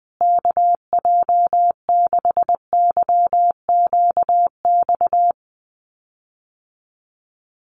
Morse code, heard as a keyed tone, about 19 wpm, 700 Hz. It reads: KJ6YQX
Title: USA Call Signs Repeated 3x 20wpm